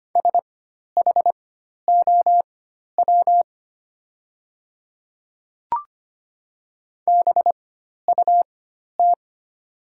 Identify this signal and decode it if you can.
Morse code, heard as a keyed tone, about 25 wpm, spaced as for 12 wpm, 700 Hz. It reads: SHOW BUT
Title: 4x Character Spacing - Top 300 Words - 25wpm